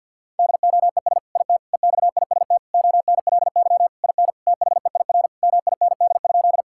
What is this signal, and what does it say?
Morse code, heard as a keyed tone, about 50 wpm, 700 Hz. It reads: DOES IT EXIST ONLY IN THEIR MIND?
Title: Sentences from Top 1000 Words - Review Part 5 - 50wpm